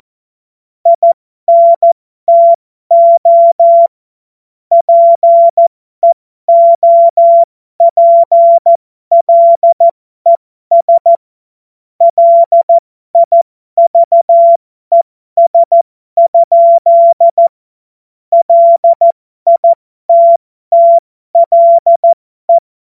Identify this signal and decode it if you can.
Morse code, heard as a keyed tone, about 14 wpm, spaced as for 12 wpm, 680 Hz. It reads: INTO PEOPLES LIVES? LITTLE
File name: Morsekurs / Witze